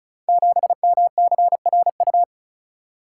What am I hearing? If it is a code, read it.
Morse code, heard as a keyed tone, about 36 wpm, 690 Hz. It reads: 7MCRU
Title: Code Groups - 5 Letter:Numbers - 35wpm